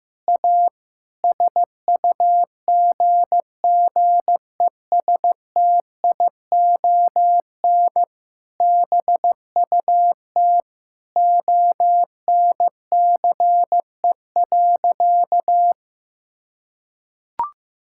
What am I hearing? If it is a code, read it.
Morse code, heard as a keyed tone, about 15 wpm, 700 Hz. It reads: A SUGGESTION BUT ONCE.
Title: Sentences from Top 1600 Words 15wpm